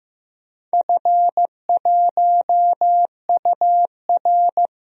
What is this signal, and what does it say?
Morse code, heard as a keyed tone, about 15 wpm, 700 Hz. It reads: F1UR
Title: DXCC-100 Call Signs - 4 Characters - 15wpm